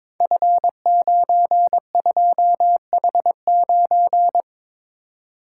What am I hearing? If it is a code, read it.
Morse code, heard as a keyed tone, about 22 wpm, 700 Hz. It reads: F92H9
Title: Code Groups - 5 Letter:Numbers - 22wpm